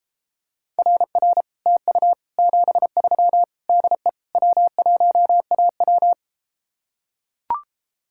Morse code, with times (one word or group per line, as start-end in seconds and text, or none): 0.8-1.4 RR
1.7-2.1 TU
2.4-3.4 73
3.7-4.1 DE
4.3-6.1 W1AW